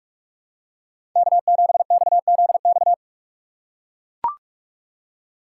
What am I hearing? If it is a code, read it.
Morse code, heard as a keyed tone, about 43 wpm, 700 Hz. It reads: K7XZX
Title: USA Call Signs - 5 Characters - 3x Character Spacing - 45wpm